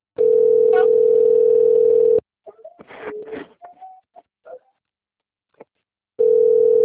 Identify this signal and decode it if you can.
Ringback tone (North America), 440 + 480 Hz.
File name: rg-1600-unknown- - - .57776